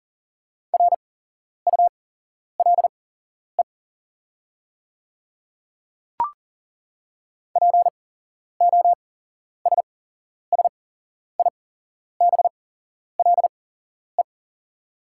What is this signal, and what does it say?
Morse code, heard as a keyed tone, about 40 wpm, spaced as for 11 wpm, 690 Hz. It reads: RULE POSSIBLE
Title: 8x Character Spacing - Top 500 Words - 40wpm